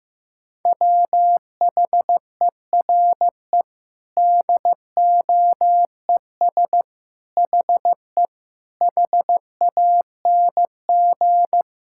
Morse code, heard as a keyed tone, about 15 wpm, 700 Hz. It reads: WHERE DOES HE HANG
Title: Sentences from Top 1600 Words 15wpm